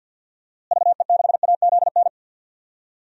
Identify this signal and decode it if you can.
Morse code, heard as a keyed tone, about 50 wpm, 700 Hz. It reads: VE6AZN
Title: DXCC-100 Call Signs - 6 Characters - Repeated 3x - 50wpm